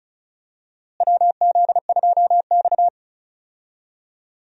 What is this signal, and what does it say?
Morse code, heard as a keyed tone, about 34 wpm, 700 Hz. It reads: WZ2X
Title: USA Call Signs - 4 Characters - Repeated 2x - 35wpm